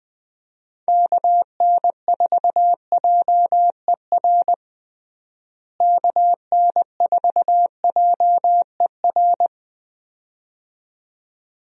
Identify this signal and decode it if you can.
Morse code, heard as a keyed tone, about 20 wpm, 700 Hz. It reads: KN4JER KN4JER
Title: USA Call Signs - 6 Characters - Repeated 2x - 20wpm